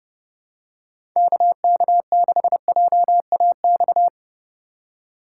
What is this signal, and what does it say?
Morse code, heard as a keyed tone, about 30 wpm, 700 Hz. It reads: KK6JAX
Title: USA Call Signs - 6 Characters - 2x Character Spacing - 30wpm